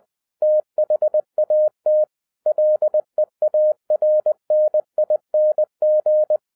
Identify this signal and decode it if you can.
Morse code, heard as a keyed tone, about 20 wpm, 610 Hz. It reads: THAT LEARNING